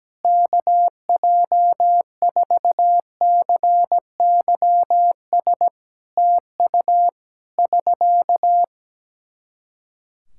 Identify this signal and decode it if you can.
Morse code, heard as a keyed tone, about 17 wpm, 700 Hz. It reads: KJ4CYS TU <SK>
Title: Practice QSO - Exchange 4 - Ending - 17wpm